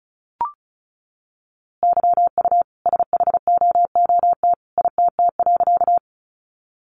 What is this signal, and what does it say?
Morse code, heard as a keyed tone, about 34 wpm, 700 Hz. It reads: YU SHOOT ITT.
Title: A Sound Of Thunder 35wpm